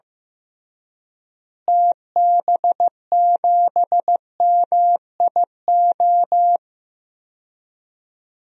Morse code, heard as a keyed tone, about 15 wpm, 700 Hz. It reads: TB7MIO
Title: DXCC-100 Call Signs - 6 Characters - Repeated 2x - 15wpm